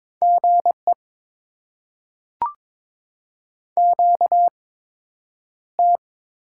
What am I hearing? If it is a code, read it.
Morse code, heard as a keyed tone, about 22 wpm, spaced as for 9 wpm, 700 Hz. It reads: GE QT
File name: Top 20 QSO Elements - 8x Character Spacing - 22wpm